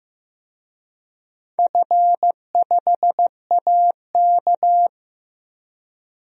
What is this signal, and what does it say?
Morse code, heard as a keyed tone, about 15 wpm, 700 Hz. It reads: F5AK
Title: DXCC-100 Call Signs - 4 Characters - Repeated 2x - 15wpm